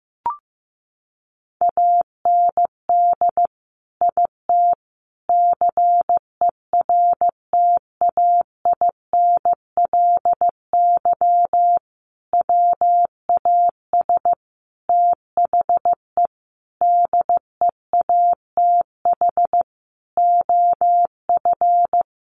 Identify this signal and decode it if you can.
Morse code, heard as a keyed tone, about 15 wpm, 700 Hz. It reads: AND IT CERTAINLY WAS THE DEATH OF
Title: Sentences from Top 1100 Words 15wpm